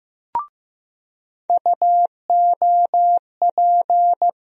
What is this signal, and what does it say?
Morse code, heard as a keyed tone, about 15 wpm, 700 Hz. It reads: UOP
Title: USA Call Sign Postfixes - 3 Letters 15wpm